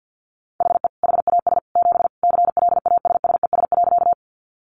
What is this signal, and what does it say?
Morse code, heard as a keyed tone, about 50 wpm, 700 Hz. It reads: HE HAS 7 CLASSES.
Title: Additional Easy Phrases 50wpm